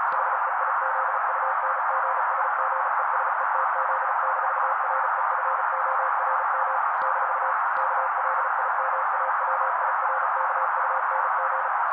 Morse code, heard as a keyed tone, about 35 wpm, 530 Hz. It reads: HEIZUNGSGESETZ-REFORM - KLIMAWIRKUNG
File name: daily / 35wpm noise